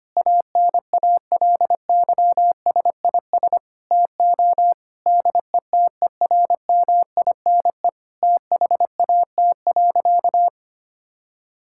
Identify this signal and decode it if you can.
Morse code, heard as a keyed tone, about 25 wpm, 700 Hz. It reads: ANALYSIS TO DETERMINE THAT.
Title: Sentences from Top 1000 Words - Review Part 5 - 25wpm